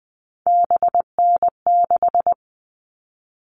Morse code, heard as a keyed tone, about 20 wpm, 700 Hz. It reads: BN6
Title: All Cut Numbers - 3-Digit 20wpm